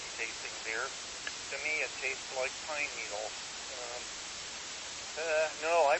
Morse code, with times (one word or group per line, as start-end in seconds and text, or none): none